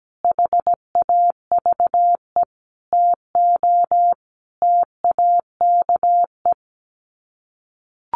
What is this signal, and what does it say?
Morse code, heard as a keyed tone, about 17 wpm, 700 Hz. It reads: HAVE TO TAKE
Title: Sets of 5 Words Encore 17wpm